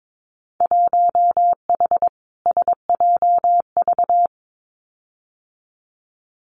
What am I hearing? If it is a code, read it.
Morse code, heard as a keyed tone, about 22 wpm, 700 Hz. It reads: 1H SJV
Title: Winter Field Day Exchanges 22wpm